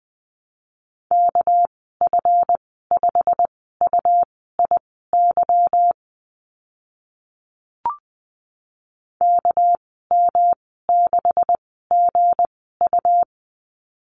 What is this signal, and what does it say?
Morse code, heard as a keyed tone, about 20 wpm, spaced as for 15 wpm, 700 Hz. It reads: KF5UIY KM6GU
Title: 231 Lesson 37g - Letter x - Call Signs 20wpm